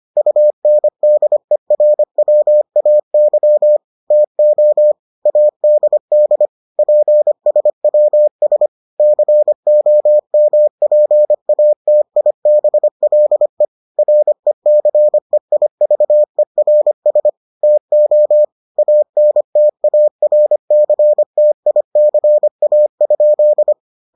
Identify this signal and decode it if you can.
Morse code, heard as a keyed tone, about 25 wpm, 600 Hz. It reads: UNDERWAY TO ADD PSWS COMPATIBLE RECEIVERS TO ANTARCTICA?